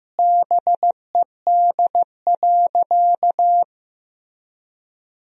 Morse code, heard as a keyed tone, about 15 wpm, 700 Hz. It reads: BED.
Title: Sentences from Top 500 Words and Abbreviations 15wpm